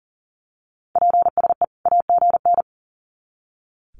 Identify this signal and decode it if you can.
Morse code, heard as a keyed tone, about 40 wpm, 700 Hz. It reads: PSE AGN
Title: Common QSO Phrases 40wpm